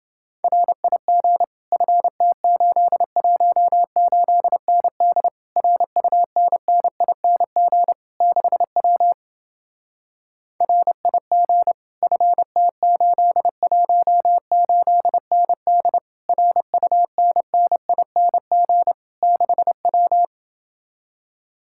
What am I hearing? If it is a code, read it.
Morse code, heard as a keyed tone, about 29 wpm, 700 Hz. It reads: RIG FT818ND RUNNING 6W RIG FT818ND RUNNING 6W